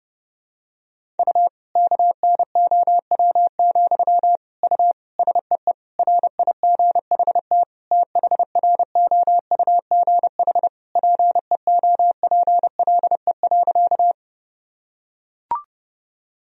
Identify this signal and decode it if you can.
Morse code, heard as a keyed tone, about 30 wpm, 700 Hz. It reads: U KNOW, U SEE RIGHT THROUGH PEOPLE.
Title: Sentences from Top 200 Words and Abbreviations 30wpm